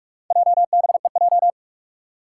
Morse code, heard as a keyed tone, about 44 wpm, 700 Hz. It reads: JBEJ
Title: Four Challenging Letter-Number Combinations 45wpm